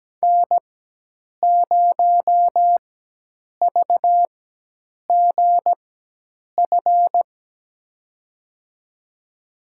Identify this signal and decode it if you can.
Morse code, heard as a keyed tone, about 17 wpm, spaced as for 8 wpm, 700 Hz. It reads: N0VGF
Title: USA Call Signs - 5 Characters - 4x Character Spacing - 17wpm